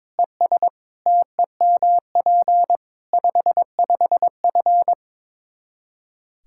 Morse code, heard as a keyed tone, about 22 wpm, 700 Hz. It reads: ES TEMP 55F
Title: Practice QSO - Exchange 2 - Info - 22wpm